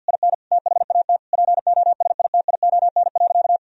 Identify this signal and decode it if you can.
Morse code, heard as a keyed tone, about 50 wpm, 700 Hz. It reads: IN THAT POSITION.
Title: Sentences from Top 1000 Words - Review Part 2 - 50wpm